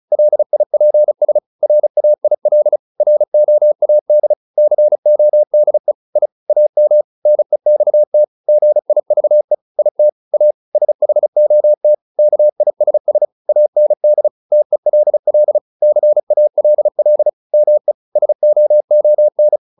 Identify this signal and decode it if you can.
Morse code, heard as a keyed tone, about 34 wpm, 600 Hz. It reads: LIPS RAIL ROAD CODE I AM NEXT GIVE IT A SHOT KISS AND TELL CALL ME SOON